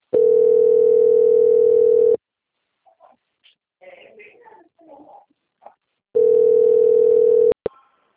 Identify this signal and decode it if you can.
Ringback tone (North America), 440 + 480 Hz, with speech.
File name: out- - - - - .454